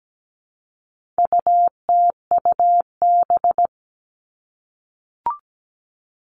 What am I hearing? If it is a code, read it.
Morse code, heard as a keyed tone, about 17 wpm, 700 Hz. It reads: UTUB
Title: All Cut Numbers - 4-Digit 17wpm